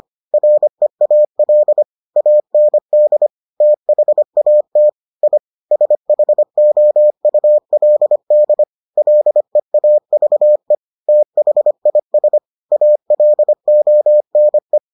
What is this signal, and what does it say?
Morse code, heard as a keyed tone, about 25 wpm, 600 Hz. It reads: REAL AND THAT I SHOULD LEAVE THIS ALONE